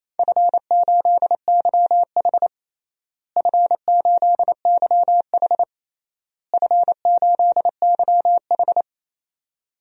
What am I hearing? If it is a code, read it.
Morse code, heard as a keyed tone, about 28 wpm, 700 Hz. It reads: F8YH F8YH F8YH
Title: DXCC-100 Call Signs - 4 Characters - Repeated 3x - 28wpm